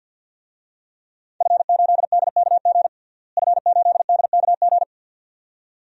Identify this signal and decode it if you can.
Morse code, heard as a keyed tone, about 52 wpm, 700 Hz. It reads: F8DKG F8DKG
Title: DXCC-100 Call Signs - 5 Characters - Repeated 2x - 50wpm